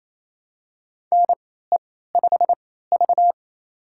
Morse code, heard as a keyed tone, about 28 wpm, spaced as for 16 wpm, 700 Hz. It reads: NE5V
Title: USA Call Signs - 4 Characters - 3x Character Spacing - 28wpm